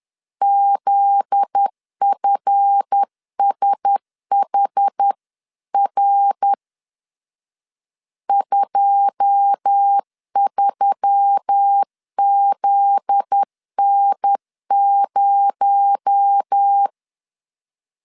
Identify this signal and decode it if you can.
Morse code, heard as a keyed tone, about 10 wpm, 800 Hz. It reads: ZFSHR 23ZN0
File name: CW / DET / 1 9 1500C S A 10wpm 4511